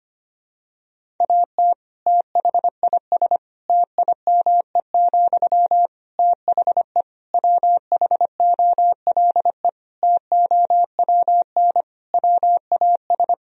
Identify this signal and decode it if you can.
Morse code, heard as a keyed tone, about 25 wpm, 700 Hz. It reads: AT THIS TIME, THE WHOLE TOWN WAS
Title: Sentences from Top 1000 Words - Review Part 2 - 25wpm